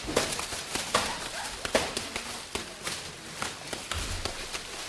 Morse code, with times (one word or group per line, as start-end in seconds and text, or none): none